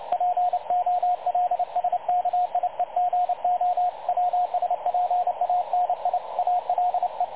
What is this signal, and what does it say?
Morse code, heard as a keyed tone, about 30 wpm, 720 Hz, with radio noise. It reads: POLSKIEGO WSPANIALE
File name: dlazaawansowanych / nagraniazpasma / websdr recording 3565.0kHz